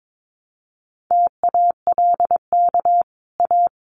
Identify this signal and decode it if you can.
Morse code, heard as a keyed tone, about 22 wpm, 700 Hz. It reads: TALK A
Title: Sets of 5 Words Encore 22wpm